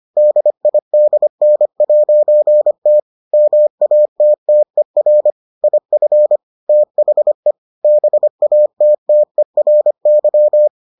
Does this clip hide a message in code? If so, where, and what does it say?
Yes: Morse code from 0.2 to 10.7 s, heard as a keyed tone, about 25 wpm, 600 Hz. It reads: DIDN'T MATTER IF THE BATTERY